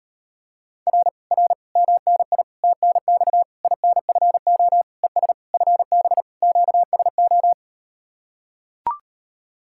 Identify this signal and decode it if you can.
Morse code, heard as a keyed tone, about 38 wpm, 700 Hz. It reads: R R MNI TNX INFO ES FB QSO